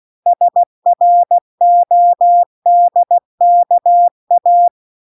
Morse code, heard as a keyed tone, about 16 wpm, 700 Hz. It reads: SRODKA